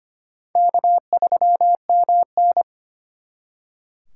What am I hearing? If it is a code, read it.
Morse code, heard as a keyed tone, about 32 wpm, 700 Hz. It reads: K3MN